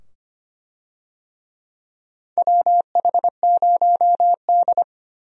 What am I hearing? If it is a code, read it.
Morse code, heard as a keyed tone, about 25 wpm, 700 Hz. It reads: WH0D
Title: USA Call Signs - 4 Characters - 25wpm